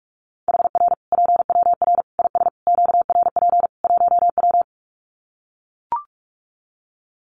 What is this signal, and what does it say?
Morse code, heard as a keyed tone, about 44 wpm, 700 Hz. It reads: HR PWR IS QRP 1W
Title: Common QSO Phrases 45wpm